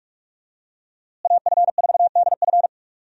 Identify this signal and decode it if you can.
Morse code, heard as a keyed tone, about 42 wpm, 700 Hz. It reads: AF4DF
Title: USA Call Signs - 5 Characters - Repeated 2x - 45wpm